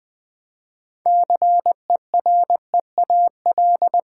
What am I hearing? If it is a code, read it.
Morse code, heard as a keyed tone, about 20 wpm, 690 Hz. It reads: CEREAL
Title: Easy Words with Hints - 20wpm